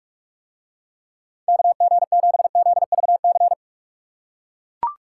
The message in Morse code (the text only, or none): KG7ZVC